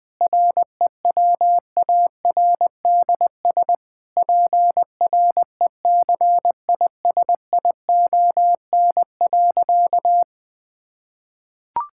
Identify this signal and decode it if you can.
Morse code, heard as a keyed tone, about 20 wpm, 700 Hz. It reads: REWARDS PRECISION.
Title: Difficult Sentences with Hints - 20wpm